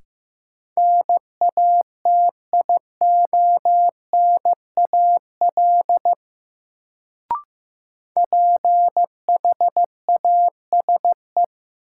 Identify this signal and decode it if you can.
Morse code, heard as a keyed tone, about 15 wpm, 700 Hz. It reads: NATIONAL PHASE
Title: Sentences from Top 1200 Words - New Word Drill Alt - 15wpm